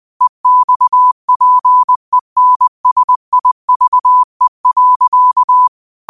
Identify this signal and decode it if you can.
Morse code, heard as a keyed tone, about 20 wpm, 1000 Hz. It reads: EXPENSIVE.
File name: qotdfiles / 20wpm